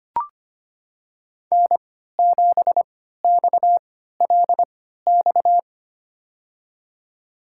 Morse code, heard as a keyed tone, about 25 wpm, spaced as for 14 wpm, 700 Hz. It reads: N7XLX
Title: USA Call Signs - 5 Characters - 3x Character Spacing - 25wpm